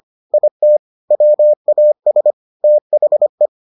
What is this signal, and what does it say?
Morse code, heard as a keyed tone, about 25 wpm, 600 Hz. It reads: IT WAS THE